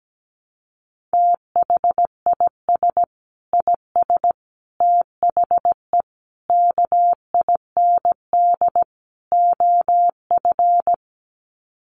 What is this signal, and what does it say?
Morse code, heard as a keyed tone, about 17 wpm, 700 Hz. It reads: THIS IS THE KIND OF
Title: Sets of 5 Words Encore 17wpm